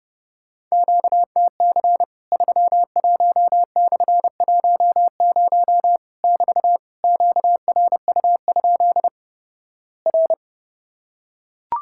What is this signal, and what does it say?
Morse code, heard as a keyed tone, about 32 wpm, 700 Hz. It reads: QTC 31/10 <BT> QRU? R
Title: Worked All Europe DX Contest - QTC Exchange - 30wpm